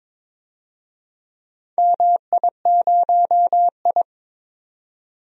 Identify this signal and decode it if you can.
Morse code, heard as a keyed tone, about 23 wpm, 700 Hz. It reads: MI0I
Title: DXCC-100 Call Signs - 4 Characters - 22wpm